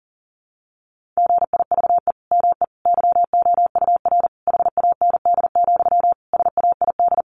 Morse code, heard as a keyed tone, about 40 wpm, 700 Hz. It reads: GIVE ME YOUR HAND, SAID
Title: Sentences from Top 1500 Words 40wpm